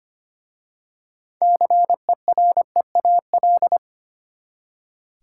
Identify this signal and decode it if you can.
Morse code, heard as a keyed tone, about 25 wpm, 700 Hz. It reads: CEREAL
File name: practice / 25wpm / Easy Words with Hints - 25wpm